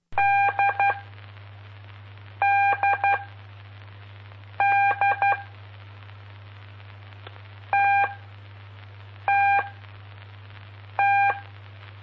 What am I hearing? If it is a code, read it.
Morse code, heard as a keyed tone, about 12 wpm, spaced as for 5 wpm, 800 Hz. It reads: DDD TTT
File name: pp5bar / aulas de telegrafia / Aula-05